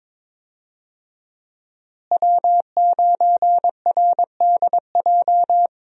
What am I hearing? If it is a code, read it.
Morse code, heard as a keyed tone, about 21 wpm, 700 Hz. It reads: W9RDJ
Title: USA Call Signs - 5 Characters - Repeated 2x - 22wpm